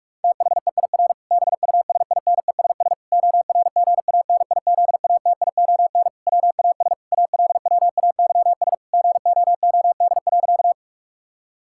Morse code, heard as a keyed tone, about 46 wpm, 700 Hz. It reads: THEIR BUSINESS ORGANIZATION WAS ALWAYS GOOD.
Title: Sentences from Top 1000 Words - Review Part 5 - 45wpm